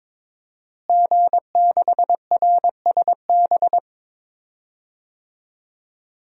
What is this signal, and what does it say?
Morse code, heard as a keyed tone, about 22 wpm, 700 Hz. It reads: G6RSB
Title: DXCC-100 Call Signs - 5 Characters - Repeated 3x - 22wpm